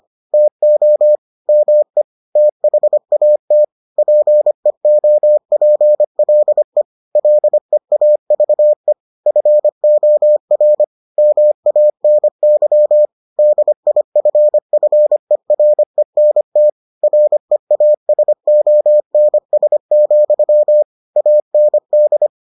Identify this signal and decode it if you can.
Morse code, heard as a keyed tone, about 25 wpm, 600 Hz. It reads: TO ME THAT PEOPLE LEAVE FOR MANY DIFFERENT REASONS, AND